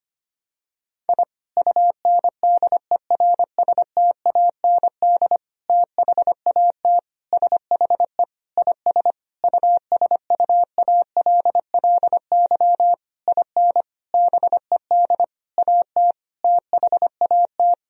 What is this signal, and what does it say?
Morse code, heard as a keyed tone, about 25 wpm, 700 Hz. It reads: I UNDERSTAND THAT SHE IS USUALLY IN BED AT THAT
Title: Sentences from Top 1000 Words - Review Part 3 - 25wpm